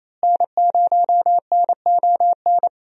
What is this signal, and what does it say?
Morse code, heard as a keyed tone, about 28 wpm, 700 Hz. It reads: N0NON
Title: USA Call Signs - 5 Characters - Repeated 3x - 28wpm